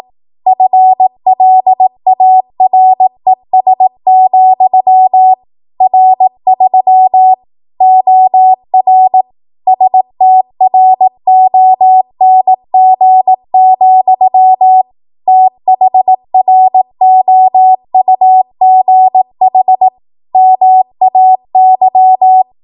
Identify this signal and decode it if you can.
Morse code, heard as a keyed tone, about 18 wpm, 750 Hz. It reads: FLARES, R3 OR STRONG, THROUGH MAY
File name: Morse / 18WPM